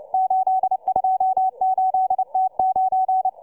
Morse code, heard as a keyed tone, about 29 wpm, 750 Hz, with radio noise. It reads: 828T9